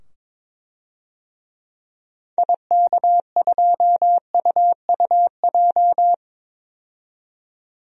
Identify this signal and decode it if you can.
Morse code, heard as a keyed tone, about 22 wpm, 700 Hz. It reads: IK2UUJ
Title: DXCC-100 Call Signs - 6 Characters - 22wpm